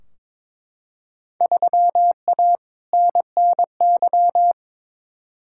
Morse code, heard as a keyed tone, about 21 wpm, 700 Hz. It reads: 3A NNY
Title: Field Day Exchanges - 22wpm